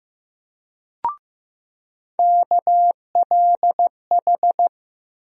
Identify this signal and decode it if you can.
Morse code, heard as a keyed tone, about 15 wpm, 700 Hz. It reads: KLH
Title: USA Call Sign Postfixes - 3 Letters 15wpm